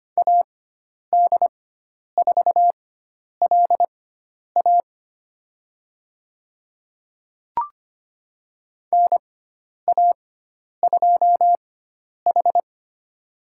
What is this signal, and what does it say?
Morse code, heard as a keyed tone, about 25 wpm, spaced as for 10 wpm, 700 Hz. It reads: AD4LA NA2H
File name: Lesson 16g - Letter c - Call Signs 25wpm